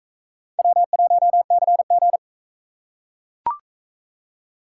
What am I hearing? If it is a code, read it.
Morse code, heard as a keyed tone, about 39 wpm, 700 Hz. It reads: W1CG